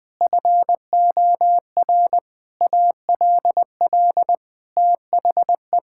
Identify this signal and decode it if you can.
Morse code, heard as a keyed tone, about 20 wpm, 700 Hz. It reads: FOR ALL THE
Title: Sets of 8 Words - 20wpm